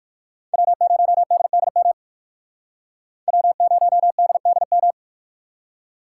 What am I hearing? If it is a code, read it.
Morse code, heard as a keyed tone, about 57 wpm, 700 Hz. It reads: W0DDM W0DDM